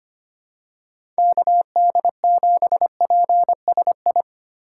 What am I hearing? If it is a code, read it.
Morse code, heard as a keyed tone, about 25 wpm, 700 Hz. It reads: KD7PSI
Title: USA Call Signs - 6 Characters - 6x Character Spacing - 25wpm